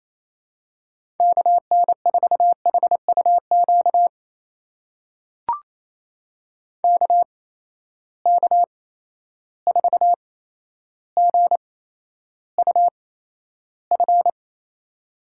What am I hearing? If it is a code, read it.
Morse code, heard as a keyed tone, about 28 wpm, spaced as for 12 wpm, 700 Hz. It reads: KN4HUQ KK4GUF
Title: USA Call Signs - 6 Characters - 8x Character Spacing - 28wpm